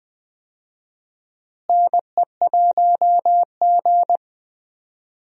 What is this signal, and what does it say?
Morse code, heard as a keyed tone, about 20 wpm, 700 Hz. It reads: NE1G
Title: USA Call Signs - 4 Characters - Repeated 2x - 20wpm